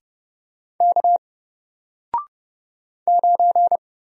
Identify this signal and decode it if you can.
Morse code, heard as a keyed tone, about 30 wpm, 700 Hz. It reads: K 9
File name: Single-Letter-Number-30wpm